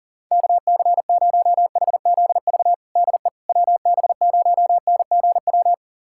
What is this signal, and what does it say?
Morse code, heard as a keyed tone, about 37 wpm, 690 Hz. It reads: KC0HZV DE WB0NGW
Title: USA Callsign DE Callsign - 6 Characters - 40wpm